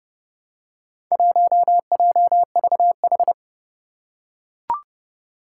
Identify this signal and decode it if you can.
Morse code, heard as a keyed tone, about 30 wpm, 700 Hz. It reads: 1JVH
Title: Four Challenging Letter-Number Combinations 30wpm